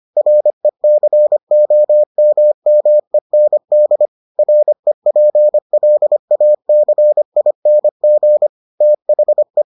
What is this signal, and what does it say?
Morse code, heard as a keyed tone, about 25 wpm, 600 Hz. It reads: RECOMMEND REPLACING THE